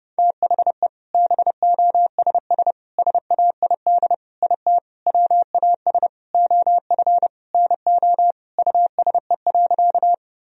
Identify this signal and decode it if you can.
Morse code, heard as a keyed tone, about 30 wpm, 700 Hz. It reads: THE BOSS SAID IT WAS OF NO USE.
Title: Sentences from Top 1000 Words - Review Part 4 - 30wpm